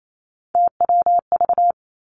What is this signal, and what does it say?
Morse code, heard as a keyed tone, about 28 wpm, 700 Hz. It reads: TWV
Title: Three Challenging Letter-Number Combinations 28wpm